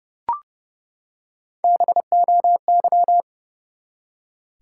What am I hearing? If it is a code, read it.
Morse code, heard as a keyed tone, about 30 wpm, 700 Hz. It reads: BOY
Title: Three Letter Words - 30wpm